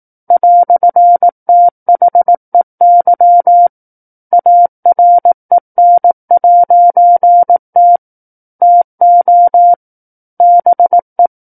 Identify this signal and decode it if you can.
Morse code, heard as a keyed tone, about 18 wpm, 700 Hz. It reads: "THEY AREN'T TO BE
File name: 18 WPM